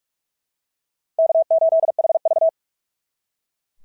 Morse code, heard as a keyed tone, about 44 wpm, 660 Hz. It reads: K8HV
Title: SOTA QSO Practice 45wpm